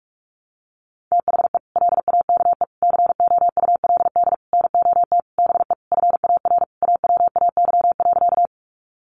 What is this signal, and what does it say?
Morse code, heard as a keyed tone, about 44 wpm, 700 Hz. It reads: THE LAKE COULD NOT BE FAR AWAY.
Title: Sentences from Top 1100 Words 45wpm